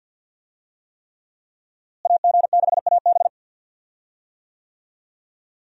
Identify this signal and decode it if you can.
Morse code, heard as a keyed tone, about 51 wpm, 700 Hz. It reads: AG6AB
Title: USA Call Signs - 5 Characters - Repeated 3x - 50wpm